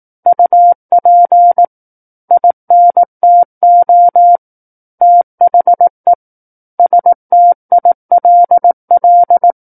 Morse code, heard as a keyed tone, about 18 wpm, 700 Hz. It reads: UP INTO THE STILL
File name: 18 WPM